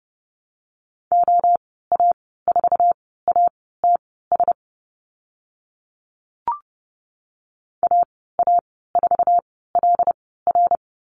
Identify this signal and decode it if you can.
Morse code, heard as a keyed tone, about 30 wpm, spaced as for 17 wpm, 700 Hz. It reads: OA4ATS AA4LR
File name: Lesson 14g - Number 2 - Call Signs 30wpm